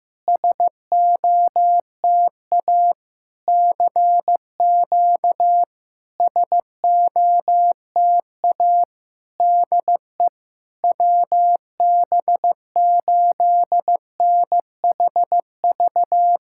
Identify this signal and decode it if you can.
Morse code, heard as a keyed tone, about 15 wpm, 700 Hz. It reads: SOTA CQ SOTA DE WB8NHV
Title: SOTA QSO Practice 15wpm